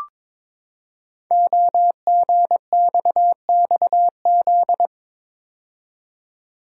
Morse code, heard as a keyed tone, about 22 wpm, 700 Hz. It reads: OGXXZ
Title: Code Groups - 5 Letter:Numbers - 22wpm